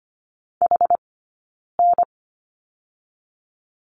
Morse code, heard as a keyed tone, about 24 wpm, 700 Hz. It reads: H N
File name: course / 25:10wpm / 092 Lesson 16d - Letter c - Two Characters 25wpm